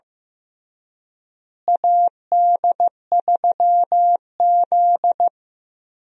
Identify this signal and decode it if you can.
Morse code, heard as a keyed tone, about 15 wpm, 700 Hz. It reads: AD3Z
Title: USA Call Signs - 4 Characters - Repeated 2x - 15wpm